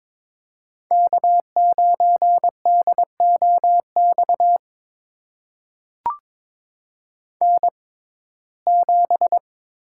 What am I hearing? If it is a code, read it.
Morse code, heard as a keyed tone, about 22 wpm, spaced as for 16 wpm, 700 Hz. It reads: K9DOX N7
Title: USA Call Signs - 5 Characters - 6x Character Spacing - 22wpm